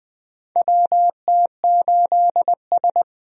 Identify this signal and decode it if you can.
Morse code, heard as a keyed tone, about 20 wpm, 700 Hz. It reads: WT8S
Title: USA Call Signs - 4 Characters - 20wpm